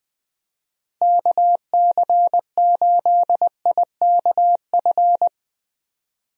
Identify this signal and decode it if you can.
Morse code, heard as a keyed tone, about 20 wpm, 700 Hz. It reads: KC8IKF
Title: USA Call Signs - 6 Characters - Repeated 3x - 20wpm